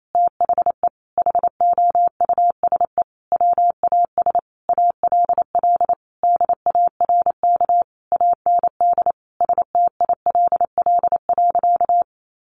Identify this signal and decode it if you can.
Morse code, heard as a keyed tone, about 28 wpm, 700 Hz. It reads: THE HOUSE WAS ALL DARK AND STILL.
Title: Sentences from Top 500 Words 28wpm